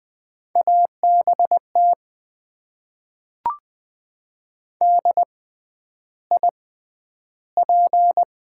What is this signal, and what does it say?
Morse code, heard as a keyed tone, about 20 wpm, spaced as for 10 wpm, 700 Hz. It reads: ABT DIP
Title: Top 90 QSO Elements - 6x Character Spacing - 20wpm